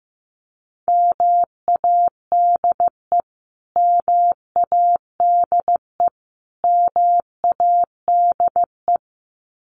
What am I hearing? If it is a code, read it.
Morse code, heard as a keyed tone, about 15 wpm, 700 Hz. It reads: MADE MADE MADE
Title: Top 300 Words Repeated 15wpm